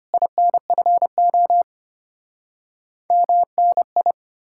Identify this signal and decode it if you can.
Morse code, heard as a keyed tone, about 33 wpm, 700 Hz. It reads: INFO MNI